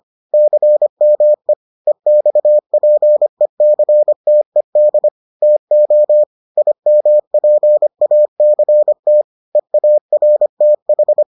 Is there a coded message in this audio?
Yes: Morse code from 0.3 to 11.2 s, heard as a keyed tone, about 25 wpm, 600 Hz. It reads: CME EXPECTED TO IMPACT EARTH